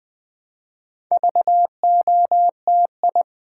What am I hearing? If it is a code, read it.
Morse code, heard as a keyed tone, about 20 wpm, 690 Hz. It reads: VOTI
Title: Build A Word - 2 Letters At A Time - 6-Letter Words - 20wpm